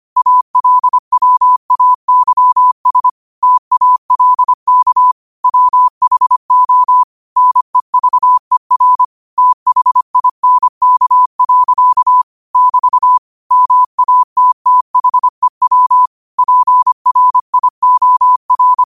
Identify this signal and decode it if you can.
Morse code, heard as a keyed tone, about 25 wpm, 1000 Hz. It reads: ALWAYS TALK WHO NEVER THINK. <BT> MATTHEW PRIOR